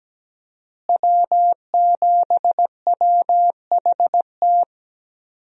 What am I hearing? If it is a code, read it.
Morse code, heard as a keyed tone, about 17 wpm, 700 Hz. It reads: W7WHT
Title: USA Call Signs - 5 Characters - Repeated 3x - 17wpm